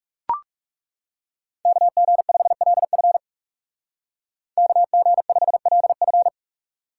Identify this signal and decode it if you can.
Morse code, heard as a keyed tone, about 36 wpm, 700 Hz. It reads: KG5LF KG5LF
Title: Speed Racing - USA 2x2 Call Signs 30wpm